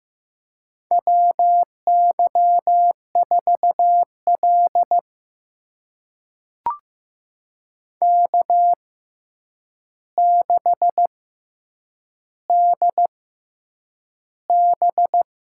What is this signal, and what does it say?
Morse code, heard as a keyed tone, about 15 wpm, spaced as for 8 wpm, 700 Hz. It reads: WY4L K6DB
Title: USA Call Signs - 4 Characters - 6x Character Spacing - 15wpm